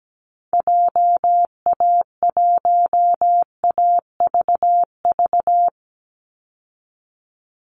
Morse code, heard as a keyed tone, about 17 wpm, 700 Hz. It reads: JA1AVV
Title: USA Call Signs 17wpm